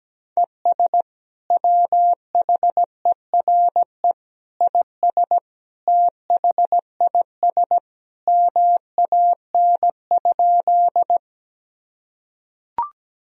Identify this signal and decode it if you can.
Morse code, heard as a keyed tone, about 17 wpm, 700 Hz. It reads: ES WHERE IS THIS MAN?
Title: Sentences from Top 200 Words and Abbreviations 17wpm